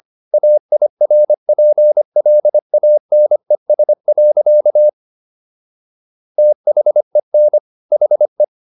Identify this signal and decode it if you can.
Morse code, heard as a keyed tone, about 25 wpm, 600 Hz. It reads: AIRPLANES. THEN HE